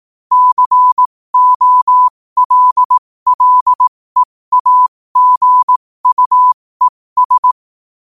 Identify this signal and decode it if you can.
Morse code, heard as a keyed tone, about 18 wpm, spaced as for 15 wpm, 1000 Hz. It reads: COLLEAGUES
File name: qotdfiles / 15wpm